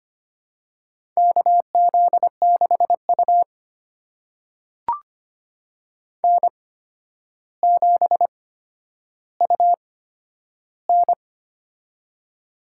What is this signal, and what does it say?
Morse code, heard as a keyed tone, about 25 wpm, spaced as for 11 wpm, 700 Hz. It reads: KZ6U N7UN
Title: USA Call Signs - 4 Characters - 8x Character Spacing - 25wpm